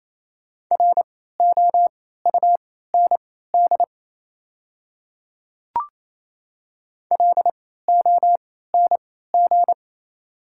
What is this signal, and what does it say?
Morse code, heard as a keyed tone, about 28 wpm, spaced as for 16 wpm, 700 Hz. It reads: ROUND LONG
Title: 3x Character Spacing - Top 500 Words - 28wpm